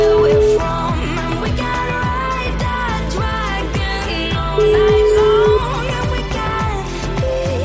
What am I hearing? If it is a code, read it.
Ringback tone (Europe, 425 Hz), with noise.